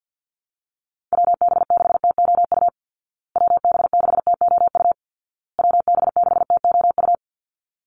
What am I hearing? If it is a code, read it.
Morse code, heard as a keyed tone, about 50 wpm, 700 Hz. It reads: WB6TOU WB6TOU WB6TOU
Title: Call Signs Repeated 2x - 50wpm